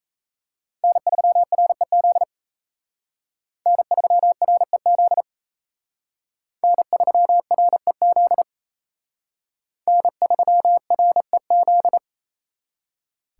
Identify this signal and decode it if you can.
Morse code, heard as a keyed tone, about 35 wpm, 700 Hz. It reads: N3REZ N3REZ N3REZ N3REZ
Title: Speed Racing - USA 1x3 Call Signs 28wpm